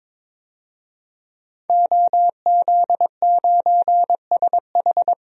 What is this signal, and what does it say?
Morse code, heard as a keyed tone, about 22 wpm, 700 Hz. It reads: OZ9SH
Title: DXCC-100 Call Signs - 5 Characters - 22wpm